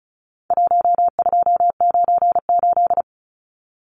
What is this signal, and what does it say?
Morse code, heard as a keyed tone, about 34 wpm, 700 Hz. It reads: 1298
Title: Numbers - 4-Digit 35wpm